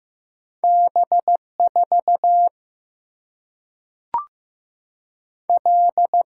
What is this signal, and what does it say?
Morse code, heard as a keyed tone, about 15 wpm, 700 Hz. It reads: B 4 L
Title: Single-Letter Single-Number 15wpm